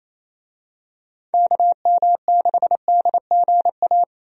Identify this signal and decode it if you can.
Morse code, heard as a keyed tone, about 28 wpm, 700 Hz. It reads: KM6DGA
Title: USA Call Signs - 6 Characters - 28wpm